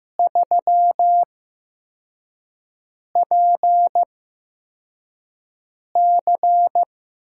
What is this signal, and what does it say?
Morse code, heard as a keyed tone, about 15 wpm, spaced as for 4 wpm, 700 Hz. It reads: 3PC
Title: USA Call Signs - 8x Character Spacing - 15wpm